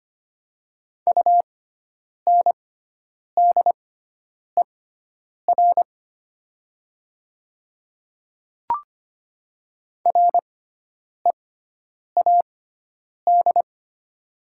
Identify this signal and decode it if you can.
Morse code, heard as a keyed tone, about 25 wpm, spaced as for 9 wpm, 700 Hz. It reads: UNDER READ
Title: 6x Character Spacing - Top 200 Words - 25wpm